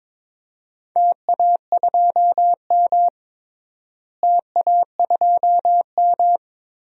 Morse code, heard as a keyed tone, about 22 wpm, 700 Hz. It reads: TA2M TA2M
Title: DXCC-100 Call Signs - 4 Characters - Repeated 3x - 22wpm